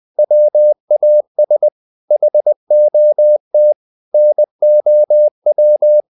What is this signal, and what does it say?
Morse code, heard as a keyed tone, about 20 wpm, 600 Hz. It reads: WAS HOT NOW
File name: the old man and the sea / the old man and the sea part 1 20wpm